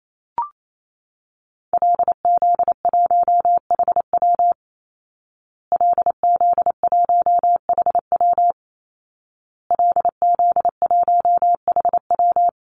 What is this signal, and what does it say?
Morse code, heard as a keyed tone, about 28 wpm, 700 Hz. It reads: LZ1HW LZ1HW LZ1HW
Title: Call Signs Repeated 2x - 28wpm